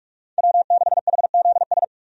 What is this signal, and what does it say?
Morse code, heard as a keyed tone, about 46 wpm, 700 Hz. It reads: W6HZS
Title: USA Call Signs - 5 Characters - 45wpm